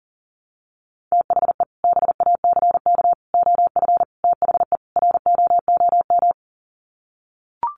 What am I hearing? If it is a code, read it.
Morse code, heard as a keyed tone, about 40 wpm, 700 Hz. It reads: THE BACK OF THE ROOM
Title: Sets of 5 Words Encore 40wpm